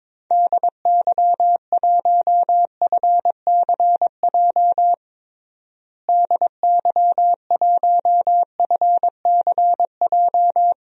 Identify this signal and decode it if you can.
Morse code, heard as a keyed tone, about 22 wpm, 700 Hz. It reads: DY1FCJ DY1FCJ
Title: DXCC-100 Call Signs - 6 Characters - Repeated 3x - 22wpm